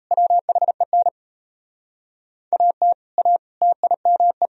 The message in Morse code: WHEN AT A TIME

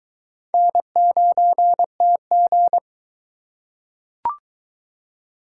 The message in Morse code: N9TG

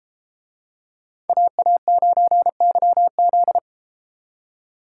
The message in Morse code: AA9YZ

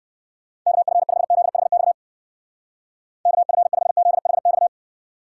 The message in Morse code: KF5ZHX KF5ZHX

